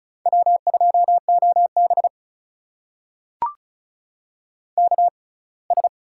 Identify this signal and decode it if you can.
Morse code, heard as a keyed tone, about 36 wpm, spaced as for 24 wpm, 700 Hz. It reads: W2OB KS
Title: USA Call Signs - 4 Characters - 6x Character Spacing - 35wpm